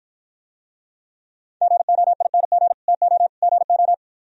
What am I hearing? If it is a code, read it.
Morse code, heard as a keyed tone, about 52 wpm, 700 Hz. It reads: GOING TO GO